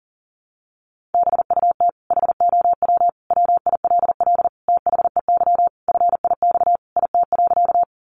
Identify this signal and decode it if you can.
Morse code, heard as a keyed tone, about 40 wpm, 700 Hz. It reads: BUT HOW WILL THEY FIX IT.
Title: Sentences from Top 1400 Words 40wpm